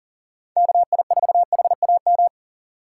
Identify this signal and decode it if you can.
Morse code, heard as a keyed tone, about 37 wpm, 700 Hz. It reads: KI4HAM
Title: USA Call Signs - 6 Characters - 3x Character Spacing - 40wpm